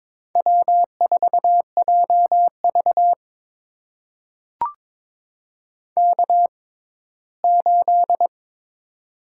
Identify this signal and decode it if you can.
Morse code, heard as a keyed tone, about 22 wpm, spaced as for 15 wpm, 700 Hz. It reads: W4JV K8
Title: USA Call Signs - 4 Characters - 6x Character Spacing - 22wpm